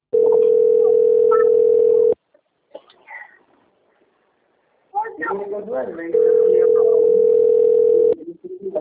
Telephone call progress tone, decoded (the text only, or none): ringback tone